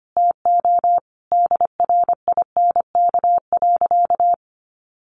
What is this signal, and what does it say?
Morse code, heard as a keyed tone, about 25 wpm, 700 Hz. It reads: TO DRINK.
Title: Sentences from Top 1500 Words 25wpm